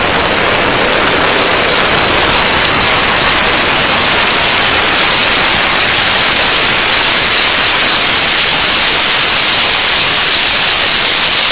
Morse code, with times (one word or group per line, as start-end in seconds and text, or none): none